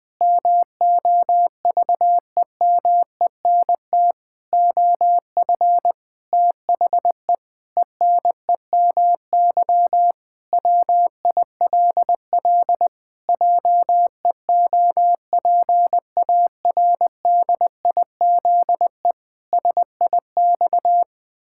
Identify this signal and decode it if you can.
Morse code, heard as a keyed tone, about 20 wpm, 700 Hz. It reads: MOVEMENT OF THE ENEMY WILL JEOPARDIZE SIX
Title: Pangram Challenge v2 - 20wpm